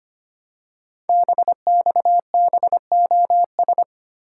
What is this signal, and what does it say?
Morse code, heard as a keyed tone, about 25 wpm, 700 Hz. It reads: BXBOS
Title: Code Groups - 5 Letters - 25wpm